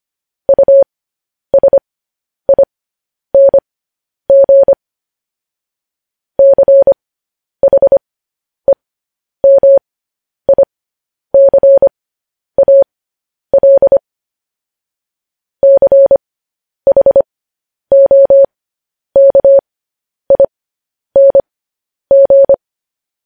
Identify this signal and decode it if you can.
Morse code, heard as a keyed tone, about 25 wpm, spaced as for 10 wpm, 550 Hz. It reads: USING CHEMICAL CHOKING
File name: podcast / News headlines-1x-25x10